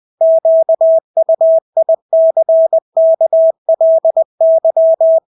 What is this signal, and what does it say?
Morse code, heard as a keyed tone, about 20 wpm, 650 Hz. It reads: QUICKLY
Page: Engelske ord 20 wpm